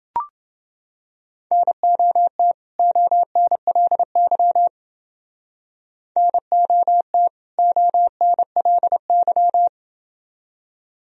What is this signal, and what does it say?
Morse code, heard as a keyed tone, about 28 wpm, 700 Hz. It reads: NOT ONLY NOT ONLY